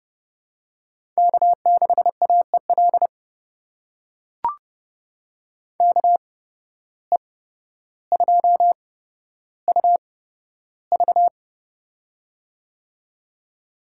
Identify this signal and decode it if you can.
Morse code, heard as a keyed tone, about 29 wpm, spaced as for 13 wpm, 700 Hz. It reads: K6AEL KE2UV
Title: USA Call Signs - 5 Characters - 8x Character Spacing - 30wpm